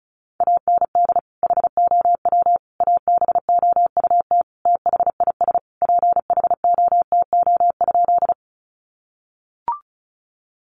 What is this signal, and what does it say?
Morse code, heard as a keyed tone, about 34 wpm, 700 Hz. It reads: AND HOW ABOUT THIS PHOTO?
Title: Sentences from Top 1200 Words 35wpm